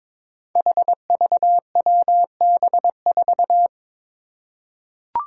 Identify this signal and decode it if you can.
Morse code, heard as a keyed tone, about 22 wpm, 700 Hz. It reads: HVWB4
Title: Five Challenging Letter-Number Combinations 22wpm